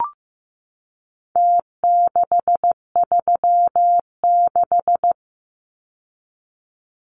Morse code, heard as a keyed tone, about 15 wpm, 700 Hz. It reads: T636
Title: Numbers - Contest Style - 4-Digit 15wpm